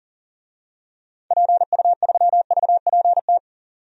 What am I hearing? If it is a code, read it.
Morse code, heard as a keyed tone, about 39 wpm, 700 Hz. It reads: PU3VPT
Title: DXCC-100 Call Signs - 6 Characters - 40wpm